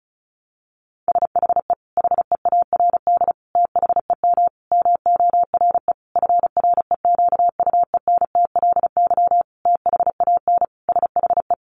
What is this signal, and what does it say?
Morse code, heard as a keyed tone, about 34 wpm, 700 Hz. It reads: SHE HEARD THEM MORE FREQUENTLY THAN SHE
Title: Sentences from Top 1300 Words 35wpm